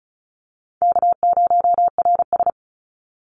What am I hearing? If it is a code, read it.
Morse code, heard as a keyed tone, about 34 wpm, 700 Hz. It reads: K0RS
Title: NA SOTA Chaser Callsigns 35wpm